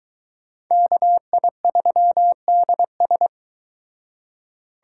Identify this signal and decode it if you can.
Morse code, heard as a keyed tone, about 22 wpm, 700 Hz. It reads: KI3DS